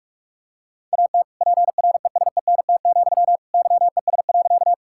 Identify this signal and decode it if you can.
Morse code, heard as a keyed tone, about 41 wpm, 700 Hz. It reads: AT PRESENT, YES.
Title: Sentences from Top 1000 Words - Review Part 5 - 45wpm